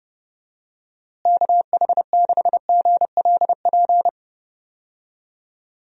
Morse code, heard as a keyed tone, about 30 wpm, 700 Hz. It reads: KH6GLP
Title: DXCC-100 Call Signs - 6 Characters - Repeated 2x - 30wpm